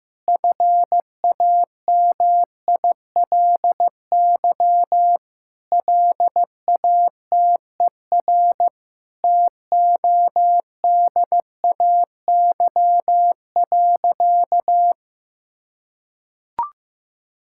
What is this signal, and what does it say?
Morse code, heard as a keyed tone, about 15 wpm, 700 Hz. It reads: FAMILY LATER TODAY.
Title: Sentences from Top 1600 Words 15wpm